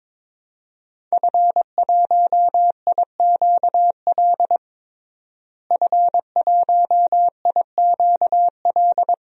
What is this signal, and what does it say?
Morse code, heard as a keyed tone, about 22 wpm, 700 Hz. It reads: F1IQL F1IQL
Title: DXCC-100 Call Signs - 5 Characters - Repeated 2x - 22wpm